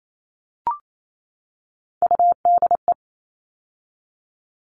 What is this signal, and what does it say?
Morse code, heard as a keyed tone, about 28 wpm, 700 Hz. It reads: UDE
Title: All Cut Numbers - 3-Digit 28wpm